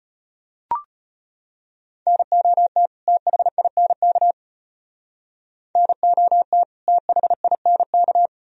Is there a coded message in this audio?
Yes: Morse code from 2.1 to 8.3 s, heard as a keyed tone, about 33 wpm, 700 Hz. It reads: NOT THINK NOT THINK